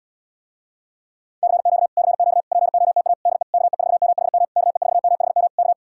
Morse code, heard as a keyed tone, about 77 wpm, 700 Hz. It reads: CQ CQ POTA DE KE4MHM KE4MHM K